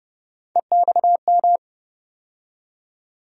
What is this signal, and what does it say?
Morse code, heard as a keyed tone, about 32 wpm, 700 Hz. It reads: EXM
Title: USA Call Sign Postfixes - 3 Letters 30wpm